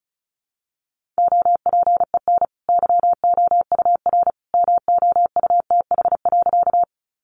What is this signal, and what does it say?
Morse code, heard as a keyed tone, about 34 wpm, 700 Hz. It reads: OPEN YOUR MOUTH.
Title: Sentences from Top 1500 Words 35wpm